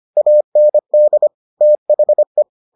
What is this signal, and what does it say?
Morse code, heard as a keyed tone, about 25 wpm, 600 Hz. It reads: AND THE